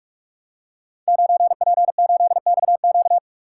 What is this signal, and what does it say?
Morse code, heard as a keyed tone, about 44 wpm, 700 Hz. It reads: 9P8XQ
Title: Code Groups - 5 Letter:Numbers - 45wpm